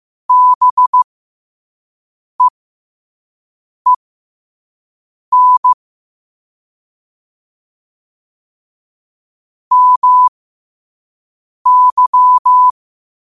Morse code, heard as a keyed tone, about 15 wpm, spaced as for 5 wpm, 1000 Hz. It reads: BEEN MY
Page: New quotes every day in morse code at 5 Words per minute.